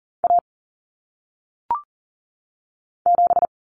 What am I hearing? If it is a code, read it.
Morse code, heard as a keyed tone, about 40 wpm, 700 Hz. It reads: A 7
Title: Common Cut Numbers - 1-Digit 40wpm